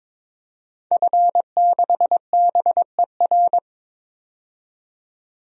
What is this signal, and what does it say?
Morse code, heard as a keyed tone, about 22 wpm, 700 Hz. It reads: F6BER
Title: DXCC-100 Call Signs - 5 Characters - 22wpm